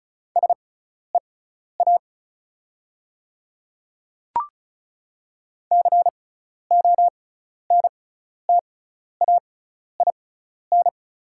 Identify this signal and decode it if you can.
Morse code, heard as a keyed tone, about 36 wpm, spaced as for 12 wpm, 690 Hz. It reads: SEA CONTAIN
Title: 6x Character Spacing - Top 500 Words - 35wpm